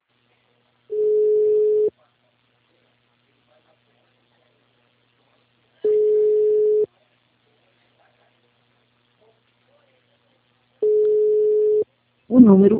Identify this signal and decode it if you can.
Ringback tone (Europe, 425 Hz).